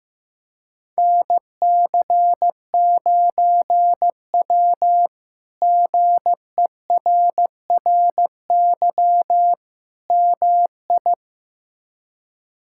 Morse code, heard as a keyed tone, about 15 wpm, 700 Hz. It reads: NC9W GERRY MI
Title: K1USN SST QSO Practice - 15wpm